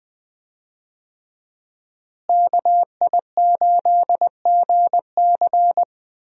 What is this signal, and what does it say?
Morse code, heard as a keyed tone, about 20 wpm, 700 Hz. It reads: KI8GC
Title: USA Call Signs - 5 Characters - Repeated 3x - 20wpm